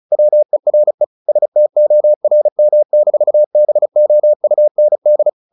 Morse code, heard as a keyed tone, about 34 wpm, 600 Hz. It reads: WERE STORM-BOUND
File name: the old man and the sea / the old man and the sea 1 35wpm